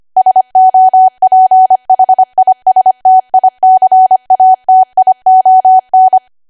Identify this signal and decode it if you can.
Morse code, heard as a keyed tone, about 25 wpm, 750 Hz. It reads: SOPHISTICATION
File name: Archive / 25 WPM / 25WPM